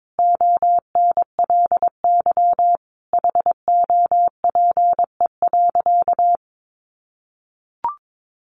Morse code, heard as a keyed tone, about 22 wpm, 700 Hz. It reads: ONLY HOPE.
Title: Sentences from Top 1000 Words 22wpm